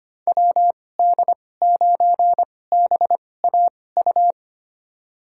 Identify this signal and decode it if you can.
Morse code, heard as a keyed tone, about 25 wpm, spaced as for 18 wpm, 700 Hz. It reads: WD9BAU
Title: USA Call Signs - 6 Characters - 2x Character Spacing - 25wpm